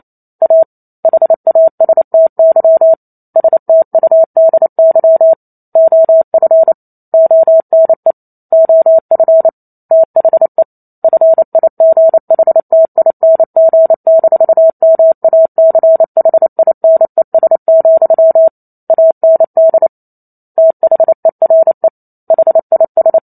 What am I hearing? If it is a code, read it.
Morse code, heard as a keyed tone, about 28 wpm, 650 Hz. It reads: A HASTY STUDY OF ONE OF THE FIGHTING-MACHINES, AND THERE HIS